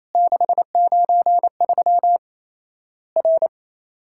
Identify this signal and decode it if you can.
Morse code, heard as a keyed tone, about 27 wpm, 700 Hz. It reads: 693 R
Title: Worked All Europe DX Contest - QTC Exchange - 28wpm